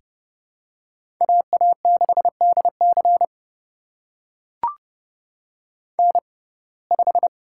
Morse code, heard as a keyed tone, about 29 wpm, spaced as for 21 wpm, 700 Hz. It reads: AA6DC N5
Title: USA Call Signs - 6x Character Spacing - 30wpm